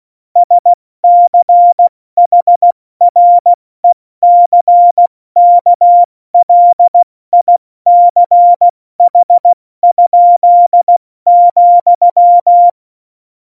Morse code, heard as a keyed tone, about 16 wpm, 700 Hz. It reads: SCHRECKLICH?,
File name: Morsekurs / Witze / Witz 15. September 2020, 70 F80